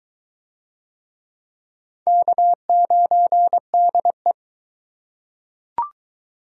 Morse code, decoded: K9DE

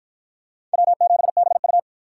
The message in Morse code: W7BU

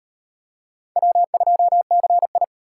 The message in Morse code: W2CI

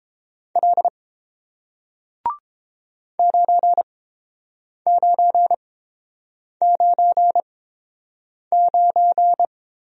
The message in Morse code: L 9999

